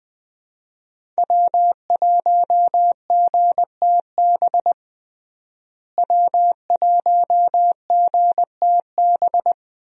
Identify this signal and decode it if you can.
Morse code, heard as a keyed tone, about 21 wpm, 700 Hz. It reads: W1GTB W1GTB
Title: USA Call Signs - 5 Characters - Repeated 2x - 20wpm